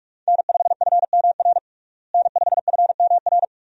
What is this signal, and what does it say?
Morse code, heard as a keyed tone, about 41 wpm, 700 Hz. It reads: N5FMR N5FMR
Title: USA Call Signs - 5 Characters - Repeated 3x - 45wpm